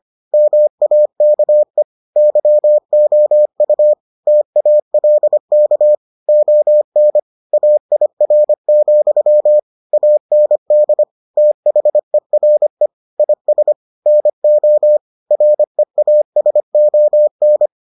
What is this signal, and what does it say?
Morse code, heard as a keyed tone, about 25 wpm, 600 Hz. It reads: MAKE YOU TALK ON AIR, AND THERE IS NO REASON